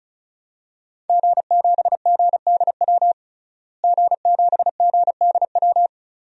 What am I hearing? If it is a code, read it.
Morse code, heard as a keyed tone, about 36 wpm, 700 Hz. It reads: G7GDW G7GDW
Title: DXCC-100 Call Signs - 5 Characters - Repeated 2x - 35wpm